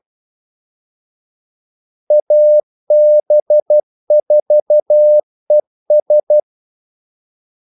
Morse code, heard as a keyed tone, about 12 wpm, 600 Hz. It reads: AB4ES
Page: 2 ekstra ordmellemrum mellem kaldesignalerne